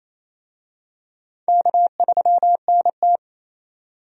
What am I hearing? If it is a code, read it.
Morse code, heard as a keyed tone, about 28 wpm, 700 Hz. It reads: K3NT
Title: USA Call Signs - 4 Characters - 28wpm